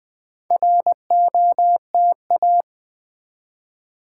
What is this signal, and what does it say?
Morse code, heard as a keyed tone, about 20 wpm, 700 Hz. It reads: ROTA
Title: Build A Word - 4 Letters At A Time - 8-Letter Words - 20wpm